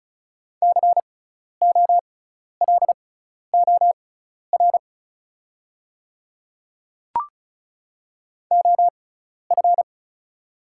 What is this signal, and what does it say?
Morse code, heard as a keyed tone, about 36 wpm, spaced as for 12 wpm, 700 Hz. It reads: COLOR OF
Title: 6x Character Spacing - Top 500 Words - 35wpm